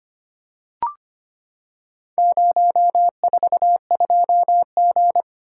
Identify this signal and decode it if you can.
Morse code, heard as a keyed tone, about 25 wpm, 700 Hz. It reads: 042G
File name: practice / 25wpm / Four Letter-Number Combinations - 25wpm